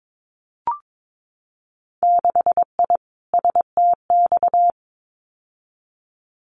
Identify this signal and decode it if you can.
Morse code, heard as a keyed tone, about 22 wpm, 700 Hz. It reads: 6I STX
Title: Winter Field Day Exchanges Encore 22wpm